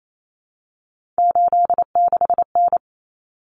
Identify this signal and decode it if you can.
Morse code, heard as a keyed tone, about 28 wpm, 700 Hz. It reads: 86N
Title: Common Cut Numbers - 3-Digit 28wpm